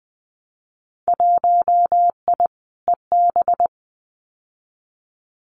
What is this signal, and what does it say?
Morse code, heard as a keyed tone, about 20 wpm, 700 Hz. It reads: 1I EB
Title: Winter Field Day Exchanges Encore 20wpm